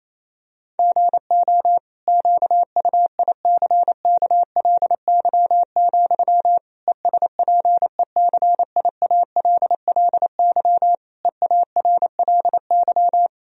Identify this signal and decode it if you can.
Morse code, heard as a keyed tone, about 28 wpm, 700 Hz. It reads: GO QUICKLY, ESPECIALLY EARLY
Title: Sentences from Top 1000 Words - Review Part 4 - 28wpm